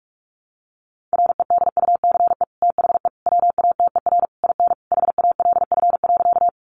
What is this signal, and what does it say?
Morse code, heard as a keyed tone, about 44 wpm, 700 Hz. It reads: REDUCE THE WATER IN HALF.
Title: Sentences from Top 1100 Words 45wpm